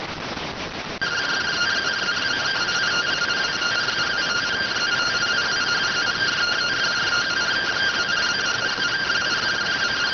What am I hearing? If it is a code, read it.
Two-tone teleprinter signal, about 170 Hz shift.